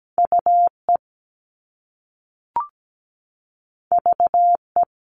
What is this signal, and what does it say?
Morse code, heard as a keyed tone, about 17 wpm, 700 Hz. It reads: UE VE
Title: All Cut Numbers - 2-Digit 17wpm